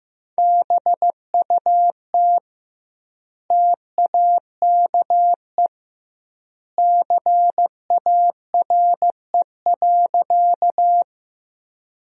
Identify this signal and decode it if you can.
Morse code, heard as a keyed tone, about 16 wpm, 700 Hz. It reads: BUT TAKE CARE.
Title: 2x Word-Spacing - Sentences from Top 300 Words - 15wpm